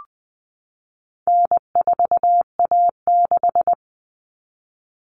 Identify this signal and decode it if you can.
Morse code, heard as a keyed tone, about 20 wpm, 700 Hz. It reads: N4A6
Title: Common Cut Numbers - 4-Digit 20wpm